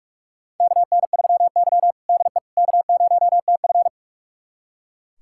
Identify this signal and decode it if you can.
Morse code, heard as a keyed tone, about 42 wpm, 700 Hz. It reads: KN3Y DE K0TF
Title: USA Callsign DE Callsign - 4 Characters - 45wpm